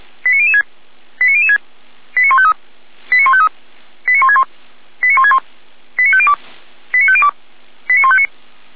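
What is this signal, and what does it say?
Five-tone selective call (ZVEI), heard as quick runs of five tones, tones of about 70 ms. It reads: 89006 89006 89153 89153 89161 89161 89502 89502 89159
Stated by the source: Five-tone ZVEI selective call received on an SDR, from audio.f85360.feuerwehr-orlamuende.org (Feuerwehr Orlamünde fire service).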